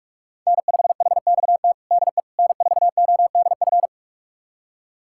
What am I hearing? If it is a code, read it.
Morse code, heard as a keyed tone, about 46 wpm, 700 Hz. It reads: N5HXT DE N4ODF
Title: USA Callsign DE Callsign - 5 Characters - 45wpm